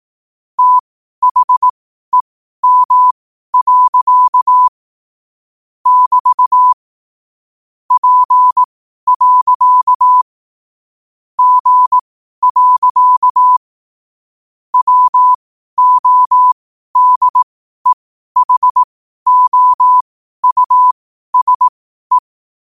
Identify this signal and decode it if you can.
Morse code, heard as a keyed tone, about 18 wpm, spaced as for 12 wpm, 1000 Hz. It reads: THEM. <BT> P. G. WODEHOUSE
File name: qotdfiles / 12wpm